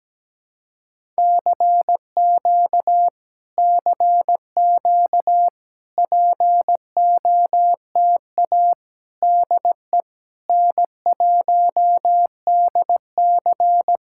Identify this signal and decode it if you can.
Morse code, heard as a keyed tone, about 17 wpm, 700 Hz. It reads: CQ CQ POTA DE N1DC
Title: Speed-Racing Common QSO Phrases - 17wpm